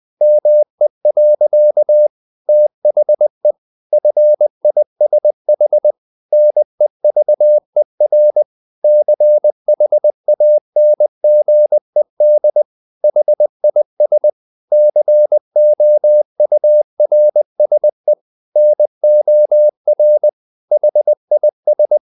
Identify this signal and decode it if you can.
Morse code, heard as a keyed tone, about 20 wpm, 600 Hz. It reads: ME. THE FISH NEVER CHANGED HIS COURSE NOR HIS
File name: the old man and the sea / the old man and the sea part 1 20wpm